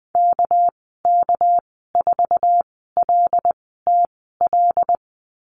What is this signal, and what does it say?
Morse code, heard as a keyed tone, about 20 wpm, spaced as for 14 wpm, 700 Hz. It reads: KK4LTL
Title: 238 Lesson 38g - Letter q - Call Signs 20wpm